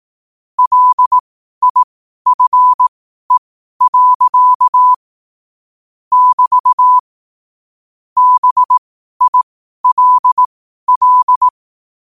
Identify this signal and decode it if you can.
Morse code, heard as a keyed tone, about 18 wpm, spaced as for 12 wpm, 1000 Hz. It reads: LIFE. <BT> BILL
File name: qotdfiles / 12wpm